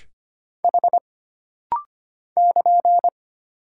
Morse code, heard as a keyed tone, about 25 wpm, 700 Hz. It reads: H <KN>
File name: Single Character (Letter:Number:Punctuation:Prosign) - Rapid-Fire - 25wpm